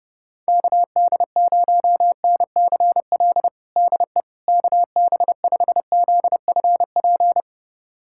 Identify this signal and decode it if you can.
Morse code, heard as a keyed tone, about 29 wpm, 700 Hz. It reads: KD0NCL DE KB5ZFP
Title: USA Callsign DE Callsign - 6 Characters - 30wpm